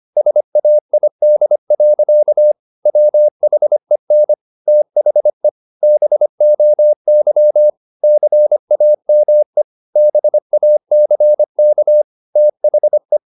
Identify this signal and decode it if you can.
Morse code, heard as a keyed tone, about 25 wpm, 600 Hz. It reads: SAID. WHEN THE BOY CAME BACK THE
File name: the old man and the sea part 1 25wpm